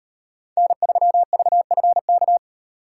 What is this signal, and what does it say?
Morse code, heard as a keyed tone, about 33 wpm, 700 Hz. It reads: N3VFK